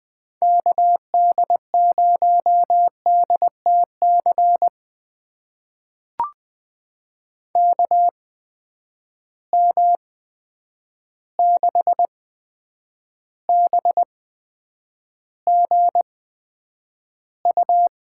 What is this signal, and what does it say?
Morse code, heard as a keyed tone, about 20 wpm, spaced as for 9 wpm, 700 Hz. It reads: KD0DTC KM6BGU
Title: USA Call Signs - 6 Characters - 8x Character Spacing - 20wpm